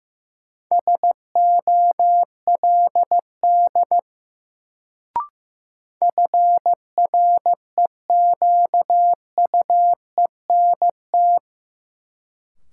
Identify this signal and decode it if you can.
Morse code, heard as a keyed tone, about 15 wpm, 700 Hz. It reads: SOLD FREQUENT
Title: Sentences from Top 1400 Words - New Word Drill - 15wpm